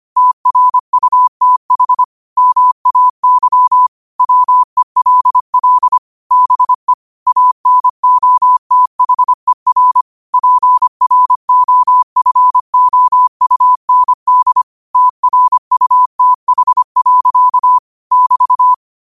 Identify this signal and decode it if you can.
Morse code, heard as a keyed tone, about 25 wpm, 1000 Hz. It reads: TRUTH MAY WELL BE ANOTHER PROFOUND TRUTH. <BT>
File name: qotdfiles / 25wpm